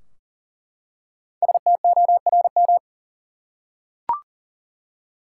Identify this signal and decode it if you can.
Morse code, heard as a keyed tone, about 37 wpm, 700 Hz. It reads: STORM
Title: Five Letter Words - 40wpm